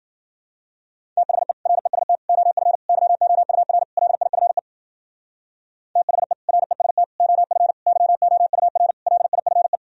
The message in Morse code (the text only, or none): THE REST OF YOUR LIFE THE REST OF YOUR LIFE